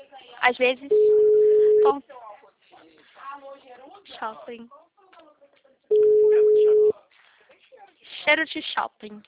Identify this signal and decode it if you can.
Ringback tone (Europe, 425 Hz), with speech.